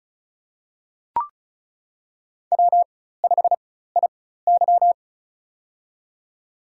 Morse code, heard as a keyed tone, about 36 wpm, spaced as for 16 wpm, 700 Hz. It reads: W5IY
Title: USA Call Signs - 4 Characters - 4x Character Spacing - 35wpm